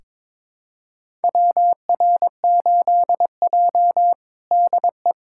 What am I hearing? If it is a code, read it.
Morse code, heard as a keyed tone, about 21 wpm, 700 Hz. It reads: WR8J DE
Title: USA Callsign DE Callsign - 4 Characters - 22wpm